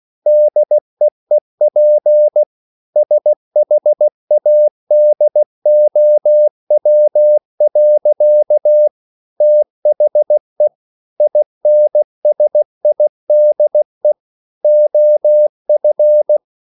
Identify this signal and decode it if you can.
Morse code, heard as a keyed tone, about 16 wpm, 600 Hz. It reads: DEEP SHADOW. THE INSIDE OF